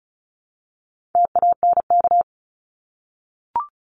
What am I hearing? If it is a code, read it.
Morse code, heard as a keyed tone, about 34 wpm, 700 Hz. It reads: TANK
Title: Top 1751-2000 Words 35wpm